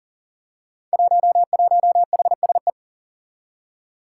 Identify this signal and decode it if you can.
Morse code, heard as a keyed tone, about 38 wpm, 700 Hz. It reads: 11HSE
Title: Five Challenging Letter-Number Combinations 40wpm